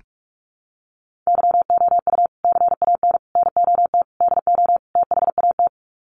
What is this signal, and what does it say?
Morse code, heard as a keyed tone, about 44 wpm, 700 Hz. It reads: YOU CAN NOT DO THAT
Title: Sets of 5 Words 45wpm